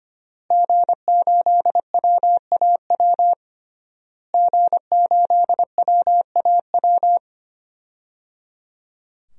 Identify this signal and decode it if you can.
Morse code, heard as a keyed tone, about 25 wpm, 700 Hz. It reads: G8WAW G8WAW
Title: DXCC-100 Call Signs - 5 Characters - Repeated 3x - 25wpm